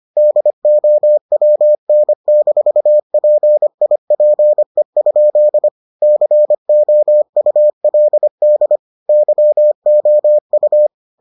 Morse code, heard as a keyed tone, about 25 wpm, 600 Hz. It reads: DOWN-PIPE? COULD YOU